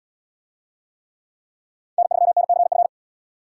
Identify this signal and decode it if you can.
Morse code, heard as a keyed tone, about 69 wpm, 700 Hz. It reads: N3NCF